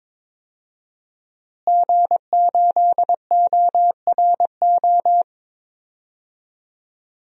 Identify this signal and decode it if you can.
Morse code, heard as a keyed tone, about 22 wpm, 700 Hz. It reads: G8ORO
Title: DXCC-100 Call Signs - 5 Characters - Repeated 3x - 22wpm